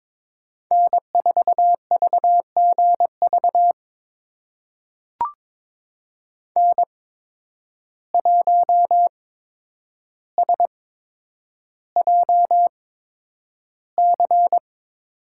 Morse code, heard as a keyed tone, about 22 wpm, spaced as for 9 wpm, 700 Hz. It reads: N4VGV N1SJC
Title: USA Call Signs - 5 Characters - 8x Character Spacing - 22wpm